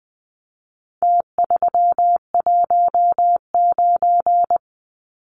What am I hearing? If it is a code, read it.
Morse code, heard as a keyed tone, about 19 wpm, 700 Hz. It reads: T319
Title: Numbers - Contest Style - 4-Digit 20wpm